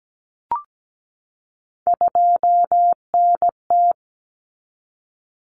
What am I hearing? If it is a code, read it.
Morse code, heard as a keyed tone, about 17 wpm, 700 Hz. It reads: 2NT
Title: Common Cut Numbers - 3-Digit 17wpm